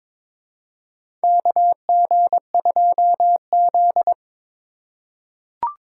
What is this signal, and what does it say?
Morse code, heard as a keyed tone, about 22 wpm, 700 Hz. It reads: KG2Z
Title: USA Call Signs - 4 Characters - 3x Character Spacing - 22wpm